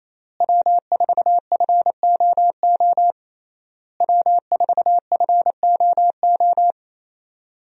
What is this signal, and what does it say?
Morse code, heard as a keyed tone, about 28 wpm, 700 Hz. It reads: W4FOO W4FOO
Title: USA Call Signs - 5 Characters - Repeated 3x - 28wpm